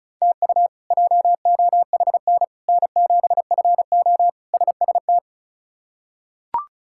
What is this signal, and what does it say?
Morse code, heard as a keyed tone, about 36 wpm, 700 Hz. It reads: TU JOHN N7FO SST
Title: K1USN SST QSO Practice - 35wpm